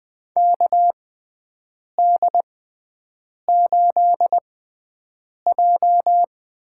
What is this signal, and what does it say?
Morse code, heard as a keyed tone, about 20 wpm, spaced as for 7 wpm, 700 Hz. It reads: KD8J
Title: USA Call Signs - 6x Character Spacing - 20wpm